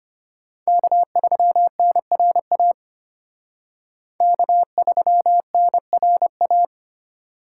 Morse code, heard as a keyed tone, about 32 wpm, 700 Hz. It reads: K3NRA K3NRA